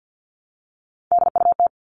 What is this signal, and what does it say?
Morse code, heard as a keyed tone, about 50 wpm, 700 Hz. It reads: DUT
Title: All Cut Numbers - 3-Digit 50wpm